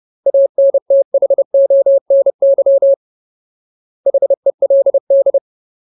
Morse code, heard as a keyed tone, about 30 wpm, 550 Hz. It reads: ANTHONY HELD
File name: Ultimate Word List / 30WPM Ultimate